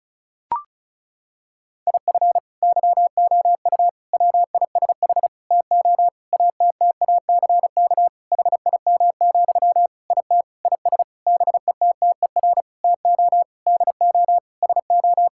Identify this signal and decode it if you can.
Morse code, heard as a keyed tone, about 34 wpm, 700 Hz. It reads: IF YOU WISH TO ATTACK HIM, IT IS BETTER TO DO SO
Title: Sentences from Top 1600 Words 35wpm